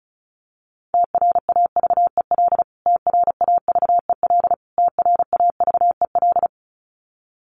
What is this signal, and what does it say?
Morse code, heard as a keyed tone, about 34 wpm, 700 Hz. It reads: TRAVEL TRAVEL TRAVEL
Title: Top 500 Words Repeated 35wpm